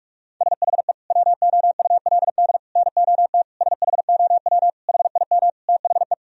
Morse code, heard as a keyed tone, about 46 wpm, 700 Hz. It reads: SHE WOULD NOT SHOW HIM THE
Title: Sentences from Top 1000 Words - Review Part 3 - 45wpm